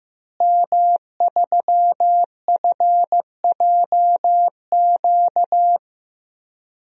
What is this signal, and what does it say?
Morse code, heard as a keyed tone, about 15 wpm, 700 Hz. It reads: M3FJQ
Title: DXCC-100 Call Signs - 5 Characters - Repeated 2x - 15wpm